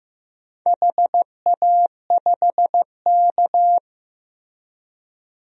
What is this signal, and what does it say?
Morse code, heard as a keyed tone, about 15 wpm, 700 Hz. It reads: HA5K
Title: DXCC-100 Call Signs - 4 Characters - Repeated 2x - 15wpm